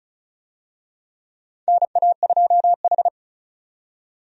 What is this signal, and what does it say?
Morse code, heard as a keyed tone, about 36 wpm, 700 Hz. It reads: NA2H
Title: USA Call Signs - 4 Characters - 35wpm